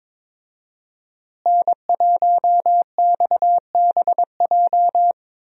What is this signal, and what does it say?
Morse code, heard as a keyed tone, about 22 wpm, 700 Hz. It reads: N1XBJ
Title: USA Call Signs - 5 Characters - Repeated 2x - 22wpm